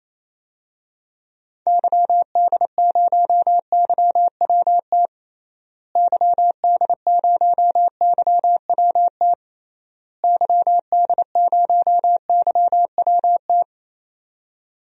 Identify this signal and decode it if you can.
Morse code, heard as a keyed tone, about 28 wpm, 700 Hz. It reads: YD0YWT YD0YWT YD0YWT
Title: DXCC-100 Call Signs - 6 Characters - Repeated 3x - 28wpm